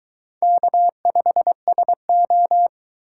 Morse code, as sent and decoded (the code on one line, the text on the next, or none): -.- ..... ... ---
K5SO